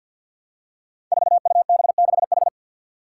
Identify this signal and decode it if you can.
Morse code, heard as a keyed tone, about 52 wpm, 700 Hz. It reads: 4UB6H
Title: Five Challenging Letter-Number Combinations 50wpm